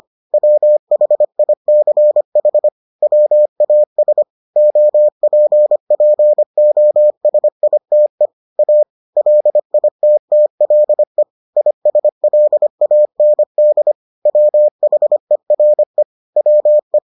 Morse code, heard as a keyed tone, about 25 wpm, 600 Hz. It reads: WHICH WAS OPPOSITE A LITTLE ISLAND WHERE WE